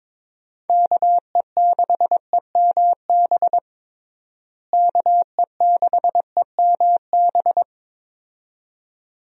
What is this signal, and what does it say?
Morse code, heard as a keyed tone, about 21 wpm, 700 Hz. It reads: KE6EMB KE6EMB
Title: USA Call Signs - 6 Characters - Repeated 2x - 22wpm